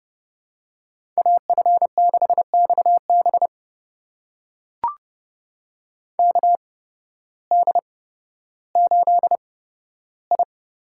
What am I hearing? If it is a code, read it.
Morse code, heard as a keyed tone, about 30 wpm, spaced as for 14 wpm, 700 Hz. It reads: AF6XB KD8I
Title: USA Call Signs - 5 Characters - 8x Character Spacing - 30wpm